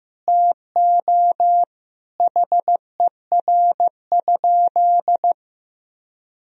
Morse code, heard as a keyed tone, about 15 wpm, 700 Hz. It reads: TO HER?
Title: Sentences from Top 1000 Words - Review Part 3 - 15wpm